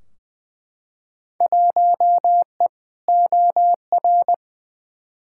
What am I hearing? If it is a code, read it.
Morse code, heard as a keyed tone, about 20 wpm, 700 Hz. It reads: 1E OR
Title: Field Day Exchanges - 20wpm